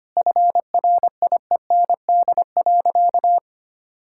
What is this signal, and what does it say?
Morse code, heard as a keyed tone, about 25 wpm, 700 Hz. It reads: FRIEND.
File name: Easy Sentences with Hints - 25wpm